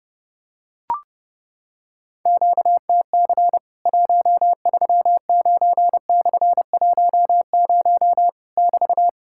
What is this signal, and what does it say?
Morse code, heard as a keyed tone, about 30 wpm, 700 Hz. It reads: QTC 139/10 <BT>
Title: Worked All Europe DX Contest - QTC Exchange - 30wpm